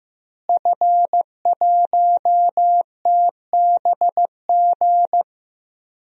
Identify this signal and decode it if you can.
Morse code, heard as a keyed tone, about 15 wpm, 700 Hz. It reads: F1TBG
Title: DXCC-100 Call Signs - 5 Characters - Repeated 3x - 15wpm